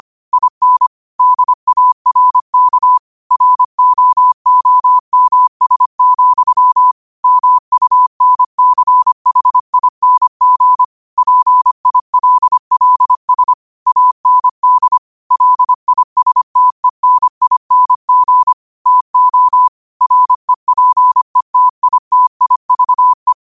Morse code, heard as a keyed tone, about 25 wpm, 1000 Hz. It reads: IN DARK ROOMS, MUNCHING PILLS AND LISTENING TO REPETITIVE
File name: qotdfiles / 25wpm